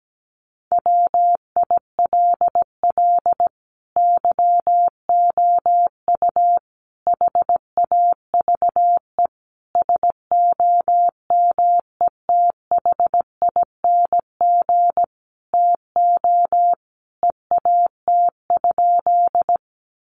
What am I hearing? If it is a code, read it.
Morse code, heard as a keyed tone, about 17 wpm, 700 Hz. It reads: WILL YOU HAVE SOMETHING TO EAT?
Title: Sentences from Top 600 Words 17wpm